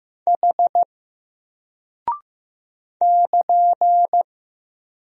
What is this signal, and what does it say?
Morse code, heard as a keyed tone, about 15 wpm, 700 Hz. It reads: H <KN>
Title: Single Character (Letter:Number:Punctuation:Prosign) 15wpm